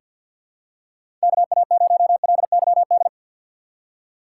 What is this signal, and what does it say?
Morse code, heard as a keyed tone, about 50 wpm, 700 Hz. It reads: KA0LYD
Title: USA Call Signs - 6 Characters - 6x Character Spacing - 50wpm